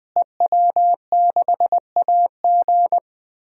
Morse code, heard as a keyed tone, about 20 wpm, 700 Hz. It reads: EW6AG
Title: DXCC-100 Call Signs - 5 Characters - Repeated 3x - 20wpm